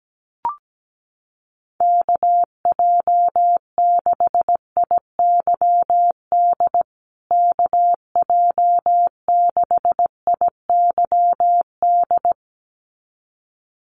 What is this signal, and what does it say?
Morse code, heard as a keyed tone, about 17 wpm, 700 Hz. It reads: KJ6IYD KJ6IYD
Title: USA Call Signs Repeated 2x 17wpm